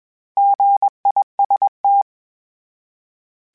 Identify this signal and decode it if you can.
Morse code, heard as a keyed tone, about 21 wpm, 800 Hz. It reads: GIST
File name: Misc / Misc-mp3s / Misc @20